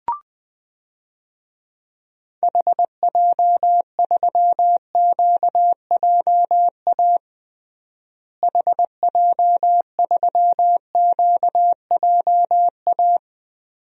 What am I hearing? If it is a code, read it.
Morse code, heard as a keyed tone, about 20 wpm, 700 Hz. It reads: HJ3QJA HJ3QJA
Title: DXCC-100 Call Signs - 6 Characters - Repeated 2x - 20wpm